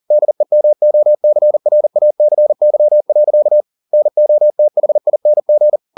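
Morse code, heard as a keyed tone, about 40 wpm, 600 Hz. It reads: DEMOCRACY. NOTHING
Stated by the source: Morse code at 40 wpm, practice text, CW med Gnister.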